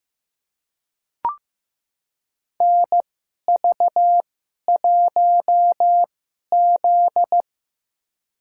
Morse code, heard as a keyed tone, about 15 wpm, spaced as for 11 wpm, 700 Hz. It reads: NV1Z
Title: USA Call Signs - 4 Characters - 2x Character Spacing - 15wpm